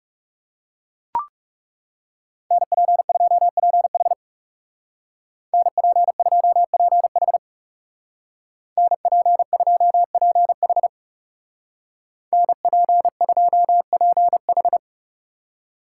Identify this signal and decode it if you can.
Morse code, heard as a keyed tone, about 35 wpm, 700 Hz. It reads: NP2PH NP2PH NP2PH NP2PH
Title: Speed Racing - USA 2x2 Call Signs 30wpm